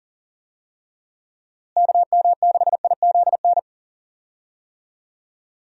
Morse code, heard as a keyed tone, about 40 wpm, 700 Hz. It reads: KM6IZN
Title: USA Call Signs - 6 Characters - Repeated 3x - 40wpm